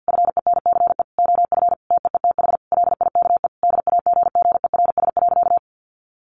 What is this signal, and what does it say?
Morse code, heard as a keyed tone, about 50 wpm, 700 Hz. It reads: FENCE OF TEETH LIKE DAGGERS.
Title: A Sound Of Thunder 50wpm